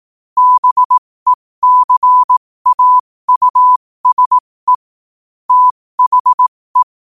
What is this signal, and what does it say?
Morse code, heard as a keyed tone, about 18 wpm, spaced as for 15 wpm, 1000 Hz. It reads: BECAUSE THE